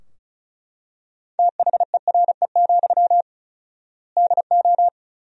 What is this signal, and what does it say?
Morse code, heard as a keyed tone, about 36 wpm, 700 Hz. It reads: THERE, DO
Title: 4x Word-Spacing - Sentences from Top 300 Words - 35wpm